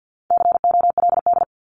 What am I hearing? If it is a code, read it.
Morse code, heard as a keyed tone, about 50 wpm, 700 Hz. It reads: COLD
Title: Top 500 Words 50wpm